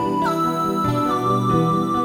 There is no code.